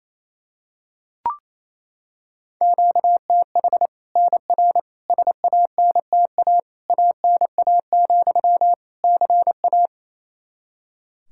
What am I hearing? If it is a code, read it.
Morse code, heard as a keyed tone, about 27 wpm, 700 Hz. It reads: QTH NR SANTA ANA, CA
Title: Practice QSO - Exchange 1 - Beginning - 28wpm